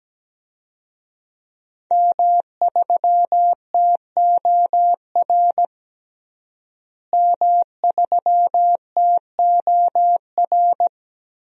Morse code, heard as a keyed tone, about 17 wpm, 700 Hz. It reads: M3TOR M3TOR
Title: DXCC-100 Call Signs - 5 Characters - Repeated 3x - 17wpm